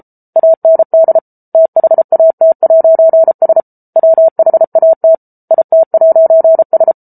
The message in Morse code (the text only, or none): AND THAT'S WHAT IT'S